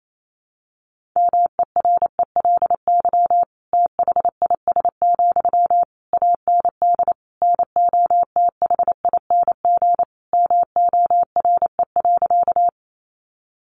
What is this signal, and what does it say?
Morse code, heard as a keyed tone, about 28 wpm, 700 Hz. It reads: MERELY THIS, AND NOTHING MORE.
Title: Sentences from Top 1400 Words 28wpm